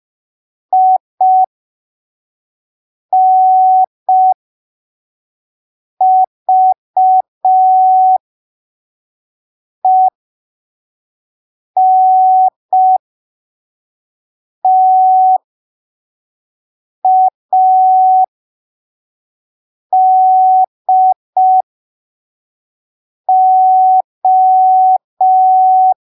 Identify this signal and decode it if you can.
Morse code, heard as a keyed tone, about 5 wpm, spaced as for 3 wpm, 750 Hz. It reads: INVENTADO
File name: arquivos / texto / texto 17 05ppm